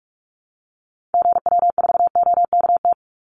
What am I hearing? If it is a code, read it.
Morse code, heard as a keyed tone, about 44 wpm, 700 Hz. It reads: GW4OKT
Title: Call Signs - 45wpm